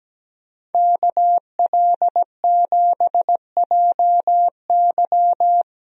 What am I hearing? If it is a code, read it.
Morse code, heard as a keyed tone, about 17 wpm, 700 Hz. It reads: KL7JY
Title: DXCC-100 Call Signs - 5 Characters - Repeated 2x - 17wpm